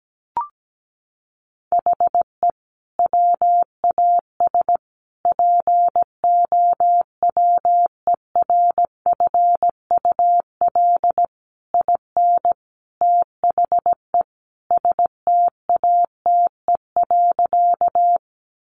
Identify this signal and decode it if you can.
Morse code, heard as a keyed tone, about 17 wpm, 700 Hz. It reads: HE WAS POWERFUL IN THE STATE.
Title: Sentences from Top 1300 Words 17wpm